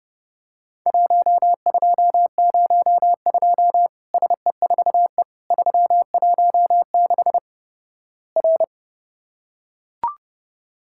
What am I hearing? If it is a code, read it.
Morse code, heard as a keyed tone, about 31 wpm, 700 Hz. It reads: 1202 SE4E 316 R
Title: Worked All Europe DX Contest - QTC Exchange - 30wpm